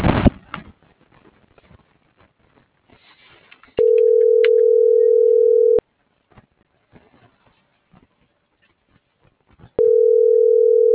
Ringback tone (North America), 440 + 480 Hz.